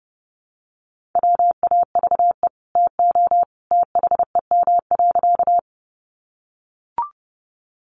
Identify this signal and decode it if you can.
Morse code, heard as a keyed tone, about 28 wpm, 700 Hz. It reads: WAVE TO THEM.
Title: Sentences from Top 1400 Words 30wpm